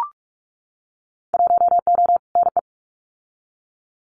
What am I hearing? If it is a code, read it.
Morse code, heard as a keyed tone, about 44 wpm, 700 Hz. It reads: 1O NE
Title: Winter Field Day Exchanges Encore 45wpm